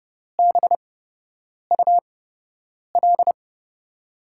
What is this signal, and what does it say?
Morse code, heard as a keyed tone, about 30 wpm, spaced as for 8 wpm, 700 Hz. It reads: BUL
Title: USA Call Signs - 5 Characters - 8x Character Spacing - 30wpm